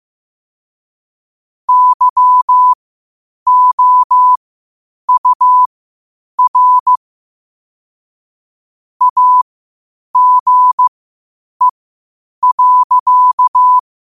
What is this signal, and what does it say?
Morse code, heard as a keyed tone, about 15 wpm, spaced as for 8 wpm, 1000 Hz. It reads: YOUR AGE.
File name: qotdfiles / 8wpm